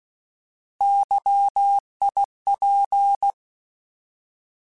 Morse code, heard as a keyed tone, about 16 wpm, 800 Hz. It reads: YIP
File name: Alphabetical / Alphabetical mp3s / Ys @15